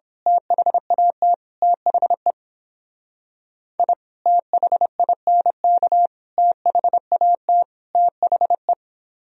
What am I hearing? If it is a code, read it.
Morse code, heard as a keyed tone, about 26 wpm, 700 Hz. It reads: THAT THE I THINK THAT THE
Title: Speed-Racing - Sets of 4 Words 22wpm